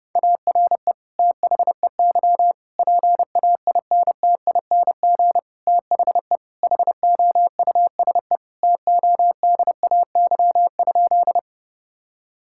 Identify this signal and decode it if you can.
Morse code, heard as a keyed tone, about 30 wpm, 700 Hz. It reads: ARE THEY PAINTING THE HOUSE TODAY?
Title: Sentences from Top 1000 Words - Review Part 6 - 30wpm